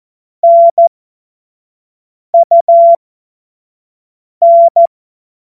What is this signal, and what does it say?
Morse code, heard as a keyed tone, about 14 wpm, spaced as for 5 wpm, 680 Hz. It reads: NUN
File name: Morsekurs / Witze / Witz 31. August 2021, 25 F70